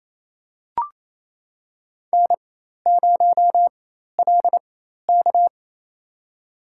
Morse code, heard as a keyed tone, about 27 wpm, spaced as for 13 wpm, 700 Hz. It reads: N0LK
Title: USA Call Signs - 4x Character Spacing - 28wpm